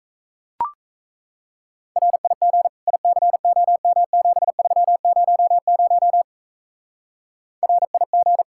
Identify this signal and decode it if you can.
Morse code, heard as a keyed tone, about 35 wpm, 700 Hz. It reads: RIG ICOM7300 RIG
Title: Speed-Racing Common QSO Phrases - 28wpm